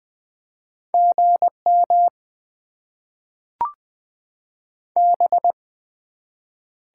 Morse code, heard as a keyed tone, about 20 wpm, 700 Hz. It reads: GM B
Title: Top 20 QSO Elements - 8x Character Spacing - 20wpm